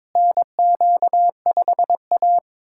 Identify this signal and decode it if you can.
Morse code, heard as a keyed tone, about 21 wpm, 700 Hz. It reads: NQ5A
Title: USA Callsign DE Callsign - 4 Characters - 22wpm